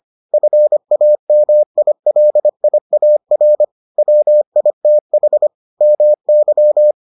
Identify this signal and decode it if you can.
Morse code, heard as a keyed tone, about 25 wpm, 600 Hz. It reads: FAMILIAR WITH MY